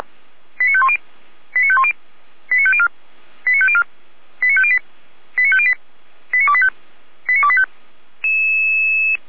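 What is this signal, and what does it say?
Five-tone selective call (ZVEI), heard as quick runs of five tones, tones of about 70 ms. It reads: 89510 89510 89504 89504 89508 89508 89285 89285
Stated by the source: ZVEI five-tone selective call received on SDR.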